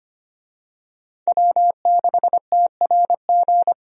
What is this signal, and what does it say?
Morse code, heard as a keyed tone, about 25 wpm, 700 Hz. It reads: W6TRG
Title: USA Call Signs - 5 Characters - Repeated 2x - 25wpm